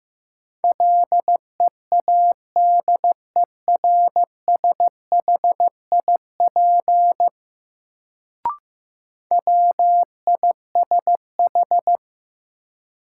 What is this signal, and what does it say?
Morse code, heard as a keyed tone, about 16 wpm, 700 Hz. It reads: LEADERSHIP WISH
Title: Sentences from Top 1600 Words - New Word Drill - 15wpm